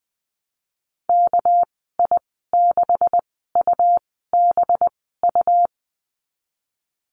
Morse code, heard as a keyed tone, about 20 wpm, spaced as for 15 wpm, 700 Hz. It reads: KI6UBU
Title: 200 Lesson 32g - Letter k - Call Signs 20wpm